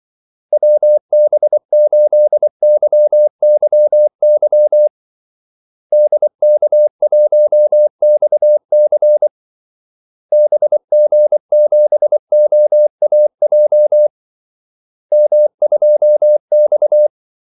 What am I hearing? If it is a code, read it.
Morse code, heard as a keyed tone, about 24 wpm, 600 Hz. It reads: WB8YYY DK1XC BG7OAJ M2X
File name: Callsigns-24wpm